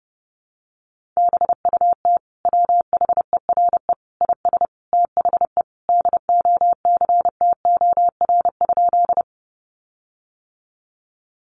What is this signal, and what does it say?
Morse code, heard as a keyed tone, about 30 wpm, 700 Hz. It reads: BUT WHERE IS THE DOCTOR?
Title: Sentences from Top 1500 Words 30wpm